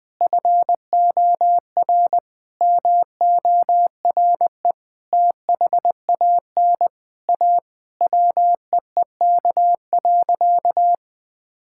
Morse code, heard as a keyed tone, about 21 wpm, 700 Hz. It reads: FOR MORE THAN A WEEK.
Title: Sentences with 4-Word Phrases - 20wpm